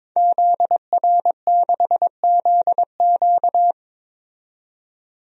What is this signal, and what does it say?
Morse code, heard as a keyed tone, about 21 wpm, 700 Hz. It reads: ZR6ZQ
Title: DXCC-100 Call Signs - 5 Characters - 22wpm